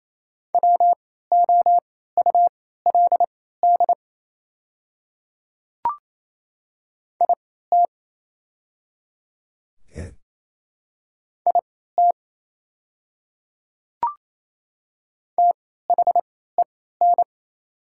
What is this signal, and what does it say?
Morse code, heard as a keyed tone, about 28 wpm, spaced as for 16 wpm, 700 Hz. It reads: WOULD IT IT THEN
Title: 3x Character Spacing - Top 100 Words - 28wpm